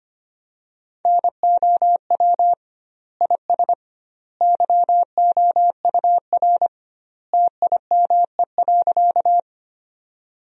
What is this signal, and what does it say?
Morse code, heard as a keyed tone, about 25 wpm, 700 Hz. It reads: NOW IS YOUR TIME.
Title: 2x Word-Spacing - Sentences from Top 100 Words - 25wpm